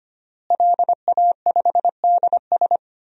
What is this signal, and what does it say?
Morse code, heard as a keyed tone, about 25 wpm, 700 Hz. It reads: LA5DS
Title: DXCC-100 Call Signs - 5 Characters - Repeated 3x - 25wpm